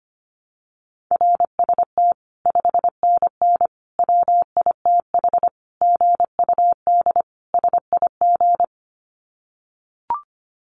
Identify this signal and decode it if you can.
Morse code, heard as a keyed tone, about 25 wpm, 700 Hz. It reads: RST 5NN WITH GUD SIG
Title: Common QSO Phrases 25wpm